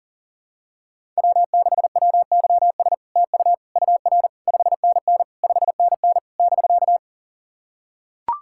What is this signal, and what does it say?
Morse code, heard as a keyed tone, about 43 wpm, 700 Hz. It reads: W6WYS TU UR 5NN 5NN <BK>
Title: SOTA QSO Practice 40wpm